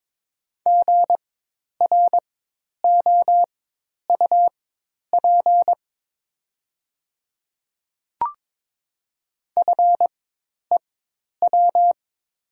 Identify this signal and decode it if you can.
Morse code, heard as a keyed tone, about 22 wpm, spaced as for 10 wpm, 700 Hz. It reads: GROUP FEW
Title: 4x Character Spacing - Top 500 Words - 22wpm